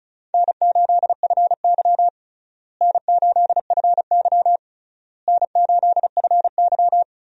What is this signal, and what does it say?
Morse code, heard as a keyed tone, about 36 wpm, 700 Hz. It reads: N8FY N8FY N8FY
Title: USA Call Signs - 4 Characters - Repeated 3x - 35wpm